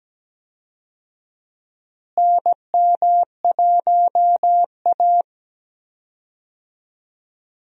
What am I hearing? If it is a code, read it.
Morse code, heard as a keyed tone, about 17 wpm, 700 Hz. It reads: NM1A
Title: USA Call Signs - 4 Characters - Repeated 2x - 17wpm